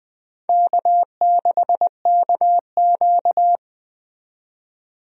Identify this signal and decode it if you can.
Morse code, heard as a keyed tone, about 20 wpm, 700 Hz. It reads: K6KQ
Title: USA Call Signs - 4 Characters - 20wpm